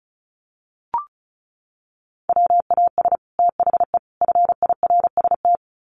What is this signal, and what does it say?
Morse code, heard as a keyed tone, about 34 wpm, 700 Hz. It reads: WAS THE FIRST
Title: Sets of 3 Words 35wpm